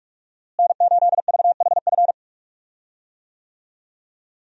Morse code, heard as a keyed tone, about 41 wpm, 700 Hz. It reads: N8VHF
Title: USA Call Signs - 5 Characters - Repeated 2x - 45wpm